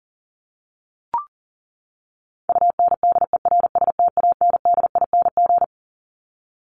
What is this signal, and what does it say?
Morse code, heard as a keyed tone, about 40 wpm, 700 Hz. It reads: UNDERSTANDING
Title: Top 501-750 Words 40wpm